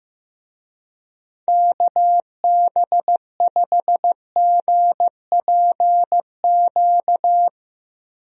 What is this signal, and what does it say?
Morse code, heard as a keyed tone, about 15 wpm, 700 Hz. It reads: KB5GPQ
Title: USA Call Signs - 6 Characters - Repeated 3x - 15wpm